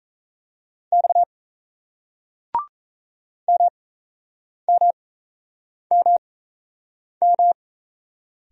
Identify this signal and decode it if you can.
Morse code, heard as a keyed tone, about 38 wpm, spaced as for 8 wpm, 700 Hz. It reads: X MMMM